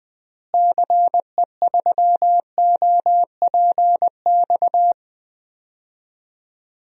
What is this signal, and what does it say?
Morse code, heard as a keyed tone, about 21 wpm, 700 Hz. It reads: CE3OPX
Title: DXCC-100 Call Signs - 6 Characters - Repeated 2x - 20wpm